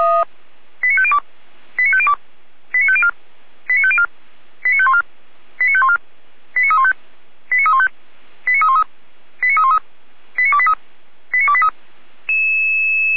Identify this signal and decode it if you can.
Five-tone selective call (ZVEI), heard as quick runs of five tones, tones of about 70 ms. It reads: 89502 89502 89504 89504 89514 89514 89315 89315 89313 89313 89283 89283